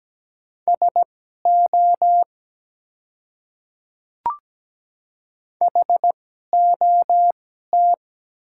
Morse code, heard as a keyed tone, about 17 wpm, spaced as for 12 wpm, 700 Hz. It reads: SO HOT
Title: 2x Character Spacing - Top 100 Words - 17wpm